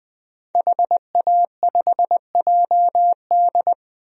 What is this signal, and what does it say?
Morse code, heard as a keyed tone, about 21 wpm, 700 Hz. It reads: HA5JD
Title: DXCC-100 Call Signs - 5 Characters - 20wpm